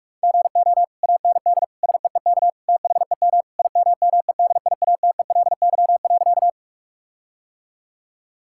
Morse code, heard as a keyed tone, about 46 wpm, 700 Hz. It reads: GO AND SEEK THEM IMMEDIATELY.
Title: Sentences from Top 1600 Words 45wpm